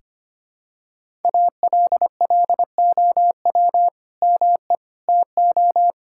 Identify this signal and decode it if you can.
Morse code, heard as a keyed tone, about 25 wpm, 700 Hz. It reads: ALLOW ME TO
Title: Sentences from Top 1000 Words - Review Part 5 - 25wpm